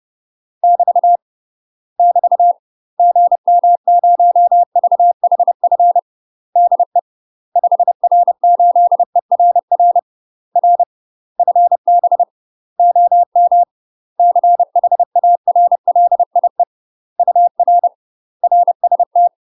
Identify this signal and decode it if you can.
Morse code, heard as a keyed tone, about 30 wpm, 700 Hz. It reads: <BT> <BT> GM0VHF DE 5R8ERR R FB OM CHARLIE UR RST
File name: UK MASTER QSOs TRAINING / 30WPM / UK MASTER QSOs 0043